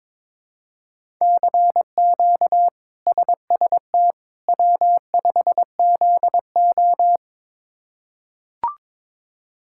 Morse code, heard as a keyed tone, about 23 wpm, 700 Hz. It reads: CQ SST W5ZO
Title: K1USN SST QSO Practice - 22wpm